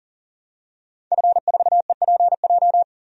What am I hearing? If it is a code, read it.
Morse code, heard as a keyed tone, about 40 wpm, 700 Hz. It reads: F4EPJ
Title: DXCC-100 Call Signs - 5 Characters - 40wpm